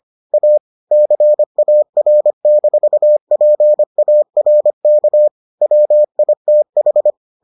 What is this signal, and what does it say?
Morse code, heard as a keyed tone, about 25 wpm, 600 Hz. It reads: A CAR-PARK WITH